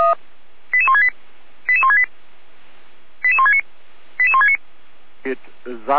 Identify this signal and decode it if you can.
Five-tone selective call (ZVEI), heard as quick runs of five tones, tones of about 70 ms, with speech. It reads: 88168 88168 88169 88169